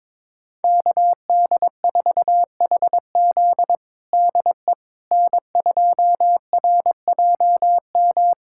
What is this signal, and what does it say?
Morse code, heard as a keyed tone, about 21 wpm, 700 Hz. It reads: KD4HZ DE N2RJM
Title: USA Callsign DE Callsign - 5 Characters - 22wpm